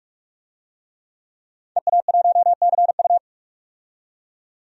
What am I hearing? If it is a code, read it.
Morse code, heard as a keyed tone, about 46 wpm, 700 Hz. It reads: EA1CU
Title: DXCC-100 Call Signs - 5 Characters - 45wpm